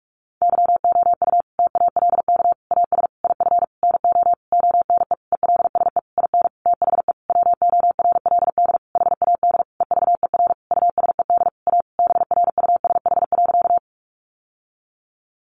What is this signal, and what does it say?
Morse code, heard as a keyed tone, about 44 wpm, 700 Hz. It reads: YOU TALK AS IF NO ONE ELSE IN THE WORLD HAD EVER USED A BRUSH.
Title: Sentences from Top 1100 Words 45wpm